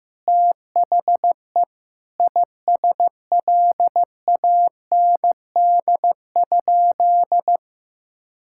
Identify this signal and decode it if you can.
Morse code, heard as a keyed tone, about 15 wpm, 700 Hz. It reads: THE ISLAND?
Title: Sentences from Top 1600 Words 15wpm